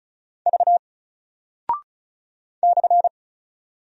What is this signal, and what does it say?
Morse code, heard as a keyed tone, about 36 wpm, 700 Hz. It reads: V /
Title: Single Character (Letter:Number:Punctuation:Prosign) 35wpm